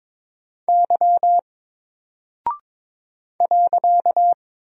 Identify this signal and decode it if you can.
Morse code, heard as a keyed tone, about 22 wpm, 700 Hz. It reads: Y .
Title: Single Character (Letter:Number:Punctuation:Prosign) 22wpm